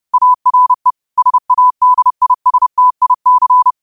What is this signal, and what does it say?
Morse code, heard as a keyed tone, about 30 wpm, 1000 Hz. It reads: ARE SADISTIC